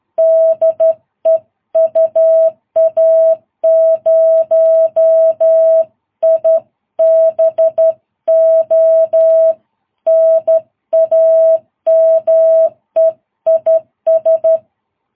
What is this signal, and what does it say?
Morse code, heard as a keyed tone, about 12 wpm, 660 Hz. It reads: DE UA0IBO NAME IS
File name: jh8cbh / 2022cw